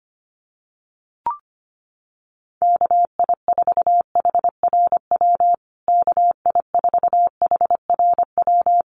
Morse code, heard as a keyed tone, about 25 wpm, 700 Hz. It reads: KI4HRW KI4HRW
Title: USA Call Signs Repeated 3x 25wpm